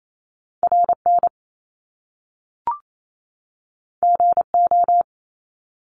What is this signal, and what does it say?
Morse code, heard as a keyed tone, about 28 wpm, 700 Hz. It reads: RN GO
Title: Top Two Letter Combinations in Words 28wpm